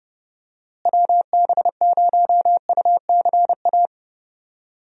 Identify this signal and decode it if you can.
Morse code, heard as a keyed tone, about 30 wpm, 700 Hz. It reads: WB0UCA
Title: USA Call Signs - 6 Characters - Repeated 2x - 30wpm